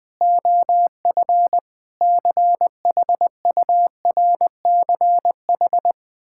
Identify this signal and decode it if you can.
Morse code, heard as a keyed tone, about 21 wpm, 700 Hz. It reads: OF CHURCH
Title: Build A Phrase - 5-Word Phrases - 20wpm